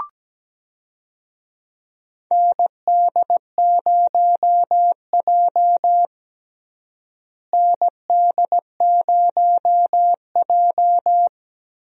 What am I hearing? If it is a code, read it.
Morse code, heard as a keyed tone, about 17 wpm, 700 Hz. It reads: ND0J ND0J
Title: USA Call Signs - 4 Characters - Repeated 2x - 17wpm